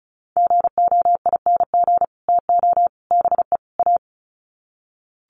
Morse code, heard as a keyed tone, about 34 wpm, 700 Hz. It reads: GOING TO BE A
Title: Sets of 4 Words 35wpm